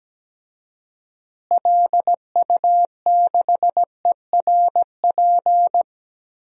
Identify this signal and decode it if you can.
Morse code, heard as a keyed tone, about 17 wpm, 700 Hz. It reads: LU6ERP
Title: DXCC-100 Call Signs - 6 Characters - Repeated 2x - 17wpm